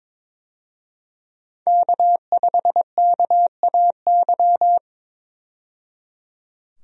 Morse code, heard as a keyed tone, about 21 wpm, 700 Hz. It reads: K5KAY
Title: USA Call Signs - 5 Characters - 22wpm